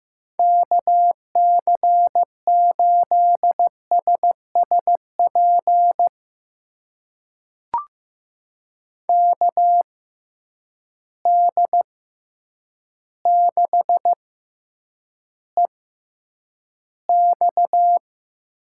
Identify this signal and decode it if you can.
Morse code, heard as a keyed tone, about 15 wpm, spaced as for 8 wpm, 700 Hz. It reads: KC8SSP KD6EX
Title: USA Call Signs - 6 Characters - 6x Character Spacing - 15wpm